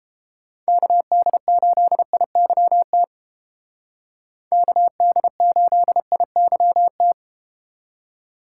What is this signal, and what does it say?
Morse code, heard as a keyed tone, about 32 wpm, 700 Hz. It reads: KD8IYT KD8IYT